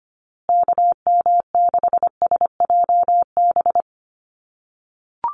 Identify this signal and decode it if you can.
Morse code, heard as a keyed tone, about 25 wpm, 700 Hz. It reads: KM6SJB
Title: USA Call Signs Repeated 3x 25wpm